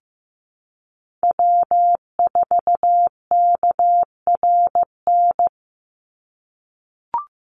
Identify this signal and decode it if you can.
Morse code, heard as a keyed tone, about 15 wpm, 700 Hz. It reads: W4KRN
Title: NA SOTA Chaser Callsigns 15wpm